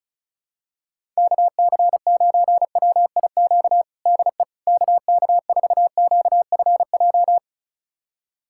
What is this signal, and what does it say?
Morse code, heard as a keyed tone, about 36 wpm, 700 Hz. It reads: KC9WIQ DE KK4QFJ
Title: USA Callsign DE Callsign - 6 Characters - 35wpm